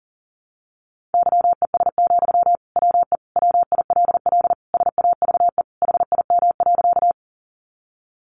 Morse code, heard as a keyed tone, about 40 wpm, 700 Hz. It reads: YES, WE WILL SAVE HIM.
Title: Sentences from Top 1100 Words 40wpm